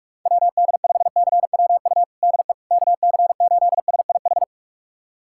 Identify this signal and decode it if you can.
Morse code, heard as a keyed tone, about 46 wpm, 700 Hz. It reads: WD5CWU DE KC8SIH
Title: USA Callsign DE Callsign 45wpm